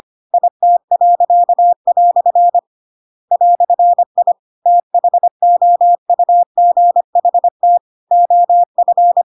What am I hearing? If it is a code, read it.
Morse code, heard as a keyed tone, about 25 wpm, 700 Hz. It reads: IT." "I THOUGHT OF
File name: Pinocchio / Pinocchio ch2 25 25 ik1hge